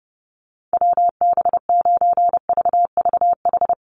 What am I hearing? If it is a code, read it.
Morse code, heard as a keyed tone, about 30 wpm, 700 Hz. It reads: WB9VVH
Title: USA Call Signs 30wpm